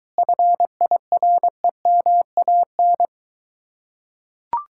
Morse code, heard as a keyed tone, about 23 wpm, 700 Hz. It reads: FIREMAN